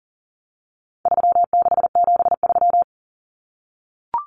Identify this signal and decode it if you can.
Morse code, heard as a keyed tone, about 40 wpm, 700 Hz. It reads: 3673
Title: Common Cut Numbers - 4-Digit 40wpm